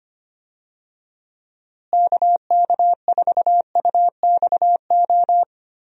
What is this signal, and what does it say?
Morse code, heard as a keyed tone, about 25 wpm, 700 Hz. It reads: KK4UXO
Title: USA Call Signs - 6 Characters - Repeated 3x - 25wpm